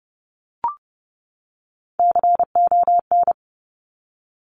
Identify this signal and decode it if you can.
Morse code, heard as a keyed tone, about 30 wpm, 700 Hz. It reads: CON
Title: Top Three Letter Combinations in Words 30wpm